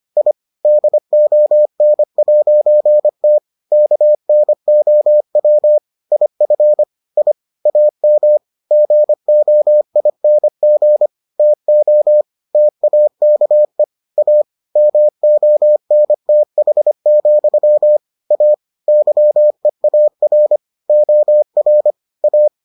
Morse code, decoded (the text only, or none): I DON'T KNOW IF I AM GOING TO TAKE A MONTH, A YEAR OR A